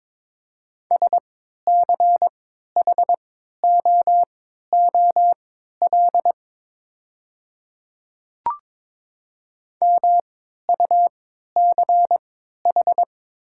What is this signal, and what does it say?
Morse code, heard as a keyed tone, about 22 wpm, spaced as for 12 wpm, 690 Hz. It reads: SCHOOL MUCH
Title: 3x Character Spacing - Top 500 Words - 22wpm